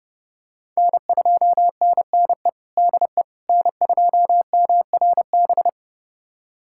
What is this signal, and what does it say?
Morse code, heard as a keyed tone, about 32 wpm, 700 Hz. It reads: N2NNE DE N2MRB
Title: USA Callsign DE Callsign - 5 Characters - 30wpm